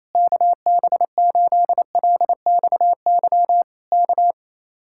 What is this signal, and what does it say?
Morse code, heard as a keyed tone, about 27 wpm, 700 Hz. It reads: KB8LXY K
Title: Speed-Racing Common QSO Phrases - 28wpm